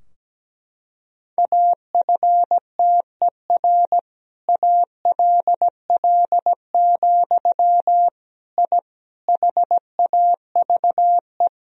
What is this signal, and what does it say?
Morse code, heard as a keyed tone, about 17 wpm, 700 Hz. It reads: AFTER ALL, I HAVE
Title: Sentences from Top 1000 Words - Review Part 3 - 17wpm